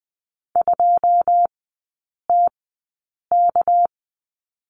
Morse code, heard as a keyed tone, about 20 wpm, spaced as for 8 wpm, 690 Hz. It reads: 2TK
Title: 198 Lesson 32e - Letter k - Three Characters 20wpm